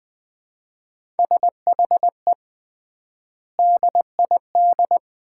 Morse code, heard as a keyed tone, about 21 wpm, 700 Hz. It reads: SHE DID
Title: 3x Word-Spacing - Sets of 4 Words - 20wpm